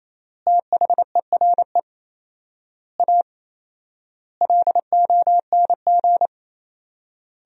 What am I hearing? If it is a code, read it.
Morse code, heard as a keyed tone, about 28 wpm, 700 Hz. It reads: THERE A LONG
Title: 4x Word-Spacing - Sentences from Top 400 Words - 28wpm